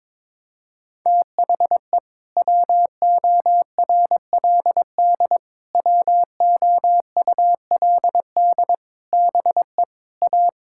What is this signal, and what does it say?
Morse code, heard as a keyed tone, about 22 wpm, 700 Hz. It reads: THE WORLD WOULD BE A
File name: practice / 22wpm / Sets of 8 Words - 22wpm